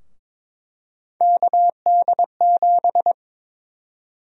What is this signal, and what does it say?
Morse code, heard as a keyed tone, about 22 wpm, 700 Hz. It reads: KD7
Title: USA Call Sign Prefixes - Repeated 2x - 22wpm